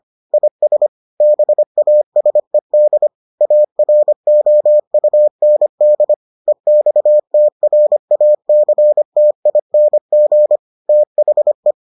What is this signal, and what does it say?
Morse code, heard as a keyed tone, about 25 wpm, 600 Hz. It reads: IS BASED AROUND EXTRACTING THE